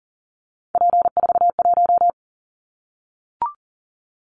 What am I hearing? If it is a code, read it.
Morse code, heard as a keyed tone, about 40 wpm, 700 Hz. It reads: P41
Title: Callsign Prefixes for Top 100 DXCC - 40wpm